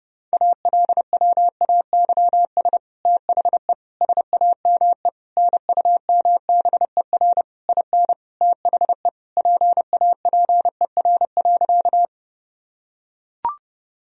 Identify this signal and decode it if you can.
Morse code, heard as a keyed tone, about 30 wpm, 700 Hz. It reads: ALWAYS THE SAME NUMBER IN THE PAPER.
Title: Sentences from Top 1000 Words - Review Part 1 - 30wpm